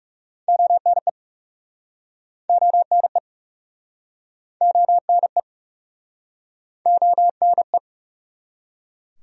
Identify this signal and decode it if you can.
Morse code, heard as a keyed tone, about 44 wpm, 700 Hz. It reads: ONE ONE ONE ONE